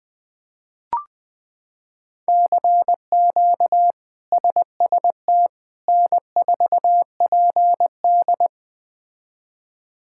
Morse code, heard as a keyed tone, about 20 wpm, 700 Hz. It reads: CQ SST N4PD
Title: K1USN SST QSO Practice - 20wpm